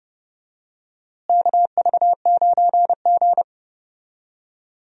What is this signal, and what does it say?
Morse code, heard as a keyed tone, about 30 wpm, 700 Hz. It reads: KV9G
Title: USA Call Signs - 4 Characters - Repeated 3x - 30wpm